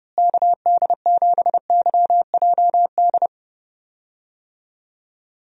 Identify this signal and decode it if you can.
Morse code, heard as a keyed tone, about 32 wpm, 700 Hz. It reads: KD7YJD
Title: USA Call Signs - 6 Characters - Repeated 2x - 30wpm